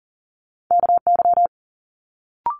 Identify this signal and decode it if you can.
Morse code, heard as a keyed tone, about 42 wpm, 700 Hz. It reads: KY
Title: Multi-letter Phonemes Quick Fire 40wpm